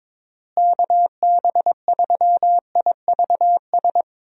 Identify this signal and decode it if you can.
Morse code, heard as a keyed tone, about 22 wpm, 700 Hz. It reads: KB3IVS
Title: USA Call Signs - 6 Characters - Repeated 2x - 22wpm